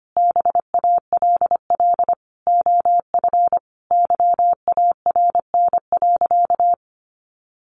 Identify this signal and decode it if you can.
Morse code, heard as a keyed tone, about 25 wpm, 700 Hz. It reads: BALL OF YARN.
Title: Additional Easy Phrases 25wpm